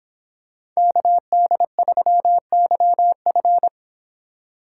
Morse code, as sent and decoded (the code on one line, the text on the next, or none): -.- -.. ...-- -.-- ..-.
KD3YF